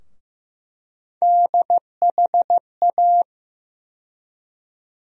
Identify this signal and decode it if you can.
Morse code, heard as a keyed tone, about 15 wpm, 700 Hz. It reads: DHA
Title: USA Call Sign Postfixes - 3 Letters 15wpm